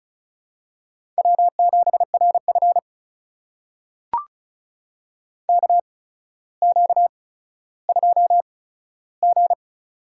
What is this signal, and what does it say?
Morse code, heard as a keyed tone, about 36 wpm, spaced as for 15 wpm, 700 Hz. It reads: W7RF KQ2G
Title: USA Call Signs - 8x Character Spacing - 35wpm